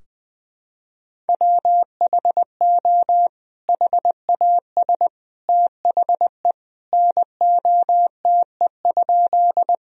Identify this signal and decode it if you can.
Morse code, heard as a keyed tone, about 20 wpm, 700 Hz. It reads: WHO HAS THE NOTE?
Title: Sentences from Top 1000 Words - Review Part 2 - 20wpm